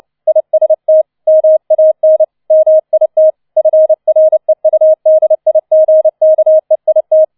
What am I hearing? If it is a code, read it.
Morse code, heard as a keyed tone, about 29 wpm, 630 Hz. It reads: IST MAN MIT FREUDIGKEIT